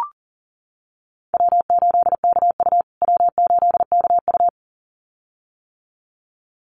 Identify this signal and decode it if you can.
Morse code, heard as a keyed tone, about 40 wpm, 700 Hz. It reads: W8KU W8KU
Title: USA Call Signs Repeated 2x 40wpm